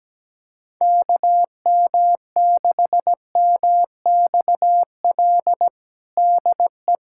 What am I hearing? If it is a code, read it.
Morse code, heard as a keyed tone, about 17 wpm, 700 Hz. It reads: KM6MXL DE
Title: USA Callsign DE Callsign 17wpm